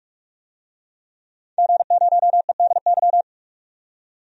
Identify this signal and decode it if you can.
Morse code, heard as a keyed tone, about 44 wpm, 700 Hz. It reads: G0EDY
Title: DXCC-100 Call Signs - 5 Characters - 45wpm